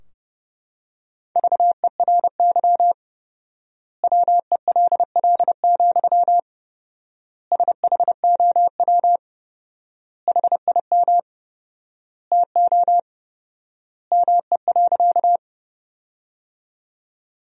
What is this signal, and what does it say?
Morse code, heard as a keyed tone, about 30 wpm, 700 Hz. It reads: VERY WELL, SHOW HIM TO ME.
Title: 4x Word-Spacing - Sentences from Top 200 Words - 30wpm